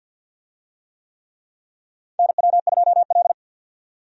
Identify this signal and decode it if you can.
Morse code, heard as a keyed tone, about 49 wpm, 700 Hz. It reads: NW2L
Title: USA Call Signs - 4 Characters - 50wpm